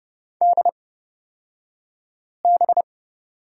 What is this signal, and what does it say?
Morse code, heard as a keyed tone, about 30 wpm, spaced as for 5 wpm, 700 Hz. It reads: DB
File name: Lesson 37d - Letter x - Two Characters 30wpm